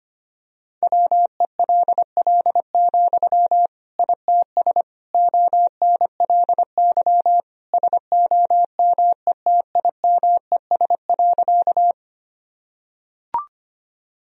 Morse code, heard as a keyed tone, about 25 wpm, 700 Hz. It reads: WELL, ITS ONLY SOMETIMES.
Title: Sentences from Top 1000 Words - Review Part 3 - 25wpm